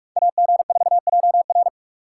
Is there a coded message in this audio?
Yes: Morse code from 0.2 to 1.7 s, heard as a keyed tone, about 42 wpm, 700 Hz. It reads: AG4JR